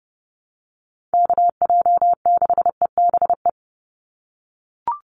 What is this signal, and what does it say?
Morse code, heard as a keyed tone, about 30 wpm, 700 Hz. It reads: KJ6EBE
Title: USA Call Signs 30wpm